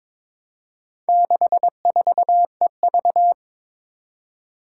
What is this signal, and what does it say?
Morse code, heard as a keyed tone, about 22 wpm, 700 Hz. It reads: 64EV
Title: Four Challenging Letter-Number Combinations 22wpm